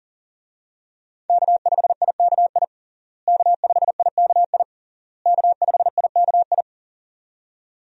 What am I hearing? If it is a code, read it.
Morse code, heard as a keyed tone, about 40 wpm, 700 Hz. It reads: K5IKI K5IKI K5IKI
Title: USA Call Signs - 5 Characters - Repeated 3x - 40wpm